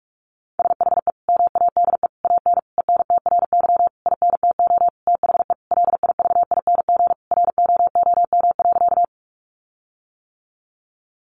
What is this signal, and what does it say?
Morse code, heard as a keyed tone, about 44 wpm, 700 Hz. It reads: SHE MADE AN ENTRY INTO THE LIVING ROOM.
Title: Sentences from Top 1500 Words 45wpm